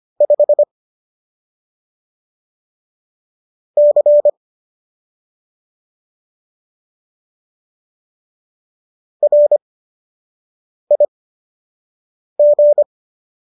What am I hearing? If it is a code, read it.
Morse code, heard as a keyed tone, about 25 wpm, spaced as for 6 wpm, 590 Hz. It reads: 5 C RIG
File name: qso012 06